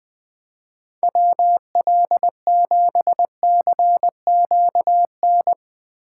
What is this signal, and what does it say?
Morse code, heard as a keyed tone, about 20 wpm, 700 Hz. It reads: WL7CQN
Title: USA Call Signs - 4x Character Spacing - 20wpm